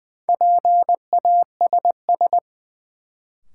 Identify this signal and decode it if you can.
Morse code, heard as a keyed tone, about 20 wpm, 700 Hz. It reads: PASS
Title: Sentences from Top 400 Words - New Word Drill - 20wpm